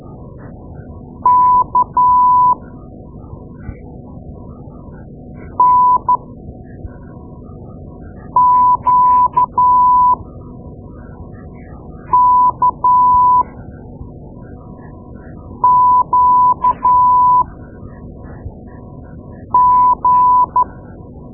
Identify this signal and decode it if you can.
Morse code, heard as a keyed tone, about 12 wpm, spaced as for 4 wpm, 1000 Hz. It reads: K NQKQG
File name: Morse Lessons / Morse Lessons / Tape-2A